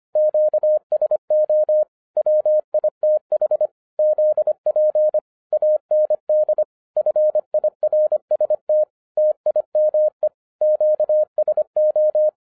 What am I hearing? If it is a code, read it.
Morse code, heard as a keyed tone, about 25 wpm, 610 Hz. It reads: QSO WITH ZP AND FIRST TIME QSO
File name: FIRST FISTS CONTACT-25wpm